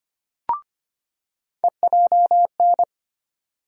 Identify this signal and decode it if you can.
Morse code, heard as a keyed tone, about 25 wpm, 700 Hz. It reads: EJN
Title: USA Call Sign Postfixes - 3 Letters 25wpm